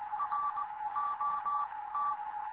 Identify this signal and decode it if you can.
Morse code, heard as a keyed tone, about 20 wpm, spaced as for 16 wpm, 1160 Hz, with noise. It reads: R OT
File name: Morze-HA2MN / hangp / 2 11 GYAKORLAT